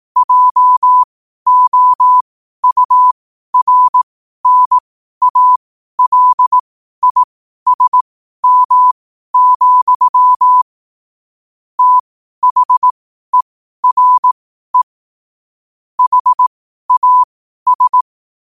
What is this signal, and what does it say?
Morse code, heard as a keyed tone, about 18 wpm, spaced as for 12 wpm, 1000 Hz. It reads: JOURNALISM, THERE HAS
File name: qotdfiles / 12wpm